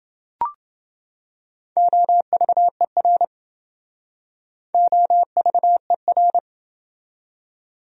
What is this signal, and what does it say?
Morse code, heard as a keyed tone, about 27 wpm, 700 Hz. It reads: OVER OVER